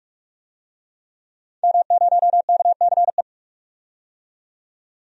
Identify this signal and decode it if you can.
Morse code, heard as a keyed tone, about 47 wpm, 700 Hz. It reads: M0KCE
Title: DXCC-100 Call Signs - 5 Characters - 45wpm